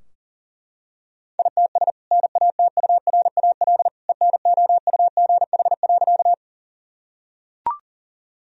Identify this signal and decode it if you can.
Morse code, heard as a keyed tone, about 37 wpm, 700 Hz. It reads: ITS NATURAL ENOUGH.
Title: Sentences from Top 1000 Words - Review Part 4 - 40wpm